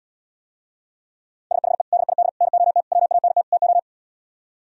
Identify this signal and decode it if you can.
Morse code, heard as a keyed tone, about 77 wpm, 700 Hz. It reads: SHE DID NOT WANT TO